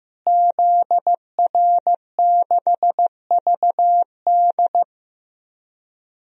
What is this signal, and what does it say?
Morse code, heard as a keyed tone, about 15 wpm, 700 Hz. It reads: ZR6VD
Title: DXCC-100 Call Signs - 5 Characters - Repeated 2x - 15wpm